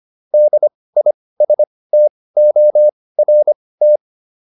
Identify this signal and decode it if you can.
Morse code, heard as a keyed tone, about 25 wpm, spaced as for 18 wpm, 600 Hz. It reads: DISTORT